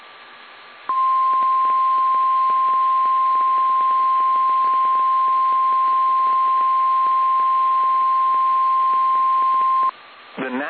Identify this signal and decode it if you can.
Emergency Alert System (SAME) tones (the 1050 Hz attention tone), with speech. It is heard from 0.9 to 10.0 s.
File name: eas / KSDZKDJL / KSDZKDJL